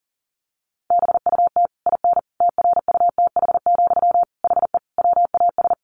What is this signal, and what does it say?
Morse code, heard as a keyed tone, about 40 wpm, 700 Hz. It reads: BUT IN TRUTH, HE WAS
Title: Sentences from Top 1500 Words 40wpm